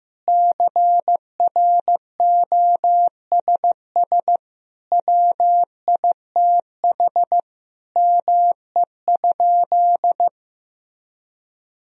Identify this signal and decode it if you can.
Morse code, heard as a keyed tone, about 15 wpm, 700 Hz. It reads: CROSS WITH ME?
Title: Sentences from Top 1000 Words - Review Part 1 - 15wpm